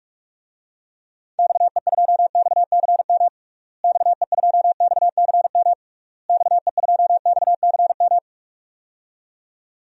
Morse code, heard as a keyed tone, about 46 wpm, 700 Hz. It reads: XE2XCM XE2XCM XE2XCM
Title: DXCC-100 Call Signs - 6 Characters - Repeated 3x - 45wpm